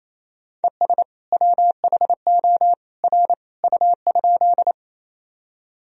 Morse code, heard as a keyed tone, about 27 wpm, 700 Hz. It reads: ES WHO R U?
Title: Sentences from Top 100 Words and Abbreviations 28wpm